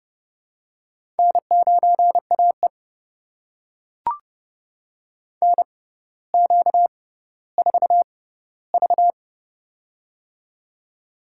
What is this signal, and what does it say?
Morse code, heard as a keyed tone, about 29 wpm, spaced as for 15 wpm, 700 Hz. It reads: N9AE NQ4V
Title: USA Call Signs - 4 Characters - 6x Character Spacing - 30wpm